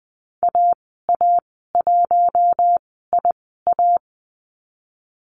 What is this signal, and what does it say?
Morse code, heard as a keyed tone, about 19 wpm, spaced as for 14 wpm, 700 Hz. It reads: AA1IA
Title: 048 Lesson 9g - Number 4 - Call Signs 20wpm